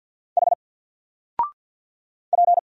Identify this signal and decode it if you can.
Morse code, heard as a keyed tone, about 50 wpm, 700 Hz. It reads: H P
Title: Single Letter-Number 50wpm